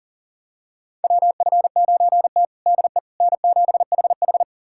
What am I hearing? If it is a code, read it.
Morse code, heard as a keyed tone, about 37 wpm, 700 Hz. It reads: WF9T DE N7HH
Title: USA Callsign DE Callsign - 4 Characters - 40wpm